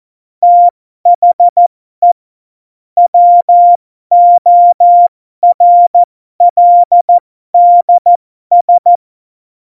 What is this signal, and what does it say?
Morse code, heard as a keyed tone, about 14 wpm, spaced as for 12 wpm, 700 Hz. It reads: THE WORLDS
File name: Morsekurs / Witze